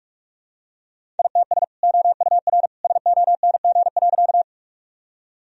Morse code, heard as a keyed tone, about 46 wpm, 700 Hz. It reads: ITS OUR SONG.
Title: Sentences from Top 1000 Words - Review Part 2 - 45wpm